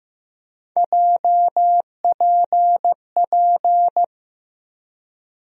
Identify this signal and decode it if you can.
Morse code, heard as a keyed tone, about 16 wpm, 700 Hz. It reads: JPP
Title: USA Call Sign Postfixes - 3 Letters 15wpm